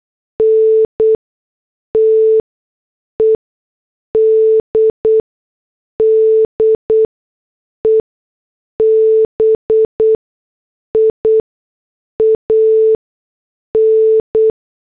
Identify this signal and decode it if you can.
Morse code, heard as a keyed tone, about 8 wpm, spaced as for 6 wpm, 440 Hz. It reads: NTED DEBIAN